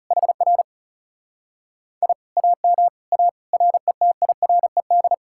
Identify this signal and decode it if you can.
Morse code, heard as a keyed tone, about 40 wpm, 700 Hz. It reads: HR I AM A RETIRED